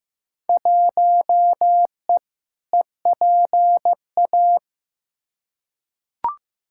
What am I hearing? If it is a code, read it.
Morse code, heard as a keyed tone, about 15 wpm, 700 Hz. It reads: 1E EPA
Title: Field Day Exchanges - 15wpm